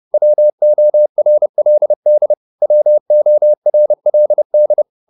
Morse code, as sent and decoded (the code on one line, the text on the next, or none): .-- --- .-. .-.. -.. / .-- --- .-. .-.. -..
WORLD WORLD